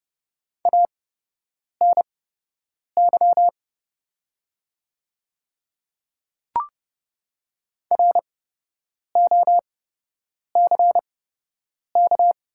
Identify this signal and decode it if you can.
Morse code, heard as a keyed tone, about 30 wpm, spaced as for 8 wpm, 700 Hz. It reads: ANY ROCK
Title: 8x Character Spacing - Top 500 Words - 30wpm